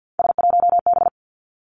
Morse code, heard as a keyed tone, about 50 wpm, 700 Hz. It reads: S1B
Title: Three Challenging Letter-Number Combinations 50wpm